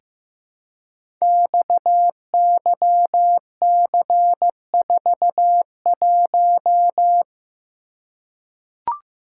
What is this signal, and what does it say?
Morse code, heard as a keyed tone, about 15 wpm, 700 Hz. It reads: XYC41
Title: Code Groups - 5 Letter:Numbers - 15wpm